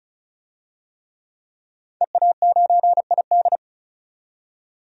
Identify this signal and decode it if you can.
Morse code, heard as a keyed tone, about 36 wpm, 700 Hz. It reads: EA9ID
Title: DXCC-100 Call Signs - 5 Characters - 35wpm